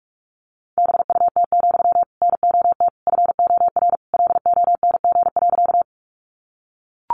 Morse code, heard as a keyed tone, about 44 wpm, 700 Hz. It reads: BUT, NOT FOR LONG.
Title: Sentences from Top 600 Words 45wpm